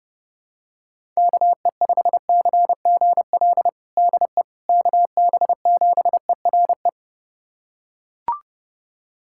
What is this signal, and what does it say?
Morse code, heard as a keyed tone, about 32 wpm, 700 Hz. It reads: KE5CGL DE KB7ERE
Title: USA Callsign DE Callsign - 6 Characters - 30wpm